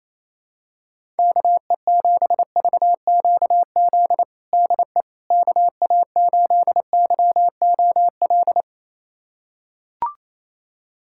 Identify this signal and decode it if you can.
Morse code, heard as a keyed tone, about 28 wpm, 700 Hz. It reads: KE7VQZ DE KA8YOL
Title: USA Callsign DE Callsign - 6 Characters - 28wpm